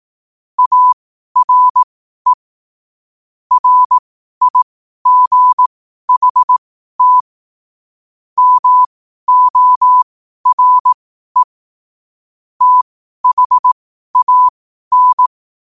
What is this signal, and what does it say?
Morse code, heard as a keyed tone, about 18 wpm, spaced as for 12 wpm, 1000 Hz. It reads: ARE RIGHT MORE THAN
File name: qotdfiles / 12wpm